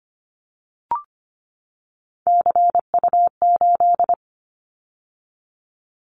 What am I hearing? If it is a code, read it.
Morse code, heard as a keyed tone, about 25 wpm, 700 Hz. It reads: CU8
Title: Callsign Prefixes for Top 50 DXCC - 25wpm